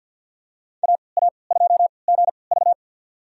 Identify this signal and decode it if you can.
Morse code, heard as a keyed tone, about 49 wpm, spaced as for 28 wpm, 700 Hz. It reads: AA2GV
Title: USA Call Signs - 5 Characters - 3x Character Spacing - 50wpm